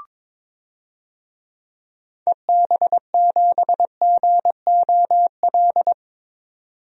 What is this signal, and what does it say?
Morse code, heard as a keyed tone, about 22 wpm, 700 Hz. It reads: EB7GOL
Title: DXCC-100 Call Signs - 6 Characters - 22wpm